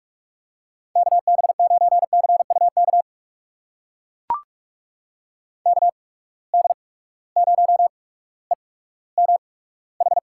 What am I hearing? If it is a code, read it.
Morse code, heard as a keyed tone, about 44 wpm, spaced as for 19 wpm, 700 Hz. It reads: KB9CUK KD0EMH
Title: USA Call Signs - 6 Characters - 8x Character Spacing - 45wpm